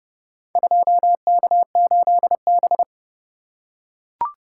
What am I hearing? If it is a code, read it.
Morse code, heard as a keyed tone, about 32 wpm, 700 Hz. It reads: 2K8B